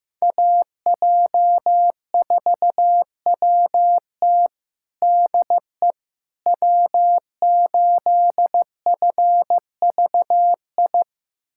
Morse code, heard as a keyed tone, about 15 wpm, 700 Hz. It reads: AJ4WT DE W8FVI
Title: USA Callsign DE Callsign - 5 Characters - 15wpm